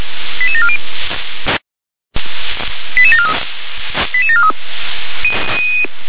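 Five-tone selective call (ZVEI), heard as quick runs of five tones, tones of about 80 ms. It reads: 88740 88743 88743
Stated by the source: ZVEI five-tone call received by SDR.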